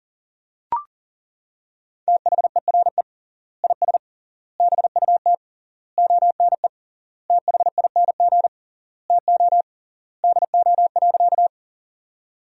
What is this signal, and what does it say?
Morse code, heard as a keyed tone, about 40 wpm, 700 Hz. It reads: THERE IS BUT ONE THING TO DO.
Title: 3x Word-Spacing - Sentences from Top 100 Words - 40wpm